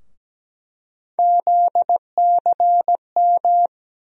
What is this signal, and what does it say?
Morse code, heard as a keyed tone, about 17 wpm, 700 Hz. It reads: ZCM
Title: USA Call Sign Postfixes - 3 Letters 17wpm